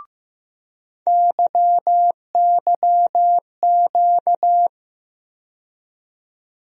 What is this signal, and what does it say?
Morse code, heard as a keyed tone, about 15 wpm, 700 Hz. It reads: YYQ
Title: USA Call Sign Postfixes - 3 Letters 15wpm